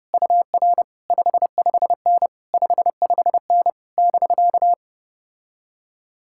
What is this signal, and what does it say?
Morse code, heard as a keyed tone, about 33 wpm, 700 Hz. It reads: UR 55N 55N <BK>